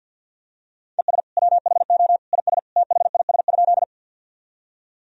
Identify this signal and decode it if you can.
Morse code, heard as a keyed tone, about 49 wpm, 700 Hz. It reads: ES WHO IS THIS?
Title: Sentences from Top 100 Words and Abbreviations 50wpm